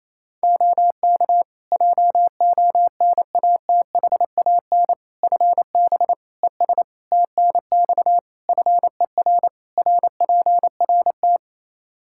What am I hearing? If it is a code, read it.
Morse code, heard as a keyed tone, about 28 wpm, 700 Hz. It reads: OK JONATHAN FB ES TNX FER RPRT
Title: Practice QSO - Exchange 2 - Info - 28wpm